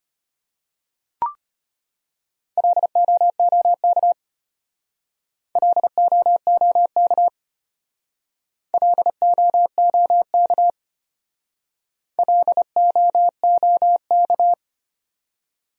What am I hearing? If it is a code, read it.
Morse code, heard as a keyed tone, about 33 wpm, 700 Hz. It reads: LOOK LOOK LOOK LOOK